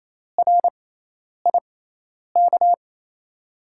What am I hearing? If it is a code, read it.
Morse code, heard as a keyed tone, about 28 wpm, spaced as for 10 wpm, 700 Hz. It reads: RIK
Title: USA Call Signs - 5 Characters - 6x Character Spacing - 28wpm